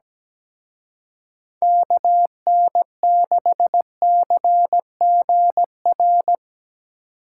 Morse code, heard as a keyed tone, about 19 wpm, 700 Hz. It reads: KN6CGR
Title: Speed Racing - USA 2x3 Call Signs 17wpm